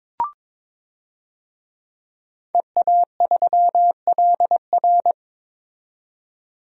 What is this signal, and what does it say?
Morse code, heard as a keyed tone, about 22 wpm, 700 Hz. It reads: EA3LR
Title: DXCC-100 Call Signs - 5 Characters - 22wpm